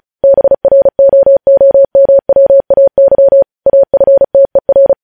Morse code, heard as a keyed tone, about 36 wpm, 550 Hz. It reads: BROOMWAY AFTER